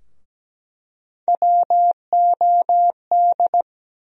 Morse code, heard as a keyed tone, about 17 wpm, 700 Hz. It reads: WOD
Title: USA Call Sign Postfixes - 3 Letters 17wpm